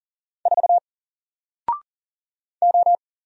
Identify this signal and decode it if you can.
Morse code, heard as a keyed tone, about 40 wpm, 700 Hz. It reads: 4 O
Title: Single Letter-Number Alt 40wpm